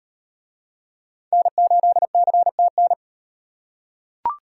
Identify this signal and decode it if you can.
Morse code, heard as a keyed tone, about 38 wpm, 700 Hz. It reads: N8CTN